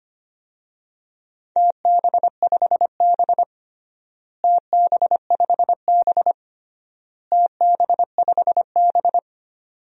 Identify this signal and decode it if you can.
Morse code, heard as a keyed tone, about 25 wpm, 700 Hz. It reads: TB5B TB5B TB5B
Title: DXCC-100 Call Signs - 4 Characters - Repeated 3x - 25wpm